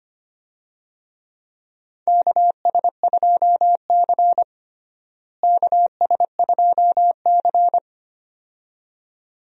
Morse code, heard as a keyed tone, about 25 wpm, 700 Hz. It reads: KS2C KS2C
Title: USA Call Signs - 4 Characters - Repeated 2x - 25wpm